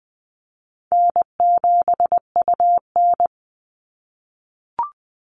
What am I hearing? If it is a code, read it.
Morse code, heard as a keyed tone, about 20 wpm, 700 Hz. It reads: N7UN
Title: NA SOTA Chaser Callsigns 20wpm